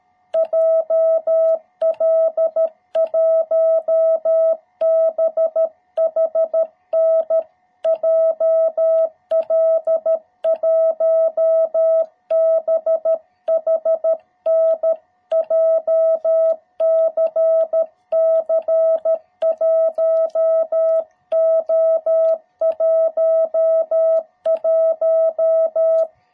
Morse code, heard as a keyed tone, about 13 wpm, 660 Hz. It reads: JL1BHN JL1BHN JCC 1O11